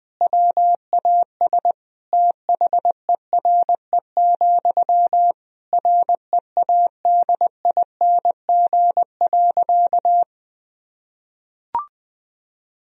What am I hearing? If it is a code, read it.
Morse code, heard as a keyed tone, about 21 wpm, 700 Hz. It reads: WAS THERE, READING.
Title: Sentences from Top 1000 Words - Review Part 3 - 20wpm